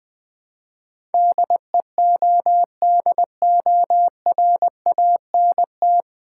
Morse code, heard as a keyed tone, about 20 wpm, 700 Hz. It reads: DEODORANT
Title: Easy Words with Hints - 20wpm